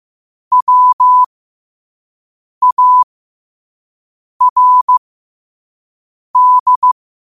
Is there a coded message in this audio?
Yes: Morse code from 0.5 to 6.9 s, heard as a keyed tone, about 15 wpm, spaced as for 5 wpm, 1000 Hz. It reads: WARD